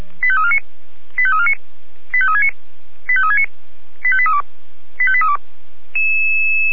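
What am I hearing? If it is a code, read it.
Five-tone selective call (ZVEI), heard as quick runs of five tones, tones of about 70 ms. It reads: 85349 85349 85369 85369 85832 85832
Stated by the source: ZVEI five-tone selective call 85839 received with an SDR.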